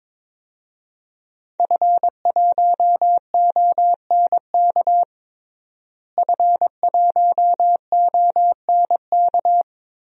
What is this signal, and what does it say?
Morse code, heard as a keyed tone, about 22 wpm, 700 Hz. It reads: F1ONK F1ONK
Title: DXCC-100 Call Signs - 5 Characters - Repeated 3x - 22wpm